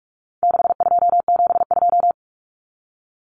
Morse code, heard as a keyed tone, about 44 wpm, 700 Hz. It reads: 6272
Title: Numbers - 4-Digit 45wpm